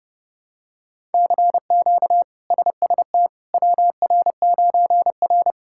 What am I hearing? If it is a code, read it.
Morse code, heard as a keyed tone, about 30 wpm, 700 Hz. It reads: CQ SST WR9R
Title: K1USN SST QSO Practice - 30wpm